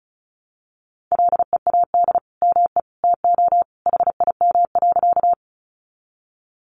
Morse code, heard as a keyed tone, about 34 wpm, 700 Hz. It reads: LEAD ME TO HIM.
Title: Sentences from Top 500 Words 35wpm